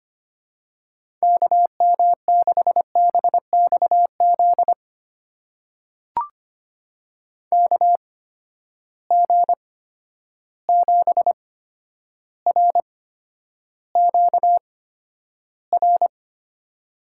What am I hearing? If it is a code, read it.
Morse code, heard as a keyed tone, about 25 wpm, spaced as for 11 wpm, 700 Hz. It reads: KM6BXZ KG7RQR
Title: USA Call Signs - 6 Characters - 8x Character Spacing - 25wpm